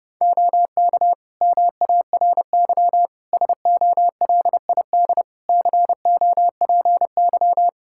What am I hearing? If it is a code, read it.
Morse code, heard as a keyed tone, about 30 wpm, 700 Hz. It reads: OK MARY SOLID COPY
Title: Practice QSO - Exchange 3 - More Info - 30wpm